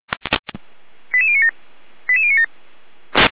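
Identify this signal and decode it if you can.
Five-tone selective call (ZVEI), heard as quick runs of five tones, tones of about 70 ms. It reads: 88097 88097
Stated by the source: ZVEI five-tone selective call received by SDR.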